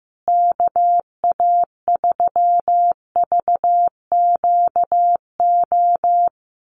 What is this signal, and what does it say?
Morse code, heard as a keyed tone, about 15 wpm, 700 Hz. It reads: KA3VQO
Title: USA Call Signs Repeated 2x 15wpm